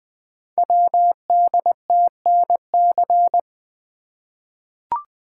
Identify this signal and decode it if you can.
Morse code, heard as a keyed tone, about 20 wpm, 700 Hz. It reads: WDTNC
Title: Code Groups - 5 Letters - 20wpm